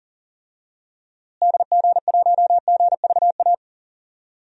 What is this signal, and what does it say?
Morse code, heard as a keyed tone, about 40 wpm, 700 Hz. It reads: DG1GVA
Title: DXCC-100 Call Signs - 6 Characters - Repeated 2x - 40wpm